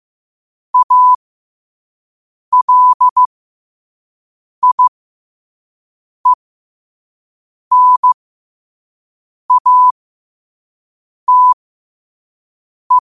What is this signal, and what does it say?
Morse code, heard as a keyed tone, about 15 wpm, spaced as for 5 wpm, 1000 Hz. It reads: ALIENATE
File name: qotdfiles / 5wpm